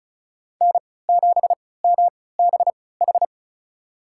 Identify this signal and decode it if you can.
Morse code, heard as a keyed tone, about 36 wpm, spaced as for 20 wpm, 700 Hz. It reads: N7MBH
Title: USA Call Signs - 5 Characters - 3x Character Spacing - 35wpm